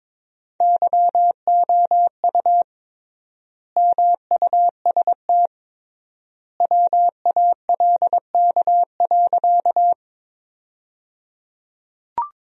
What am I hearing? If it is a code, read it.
Morse code, heard as a keyed tone, about 22 wpm, 700 Hz. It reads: YOU MUST WALK.
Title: 3x Word-Spacing - Sentences from Top 300 Words - 22wpm